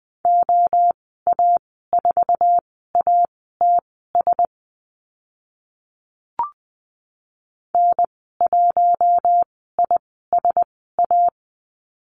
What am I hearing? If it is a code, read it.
Morse code, heard as a keyed tone, about 20 wpm, spaced as for 15 wpm, 700 Hz. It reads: OA4ATS N1ISA
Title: 088 Lesson 15g - Number 5 - Call Signs 20wpm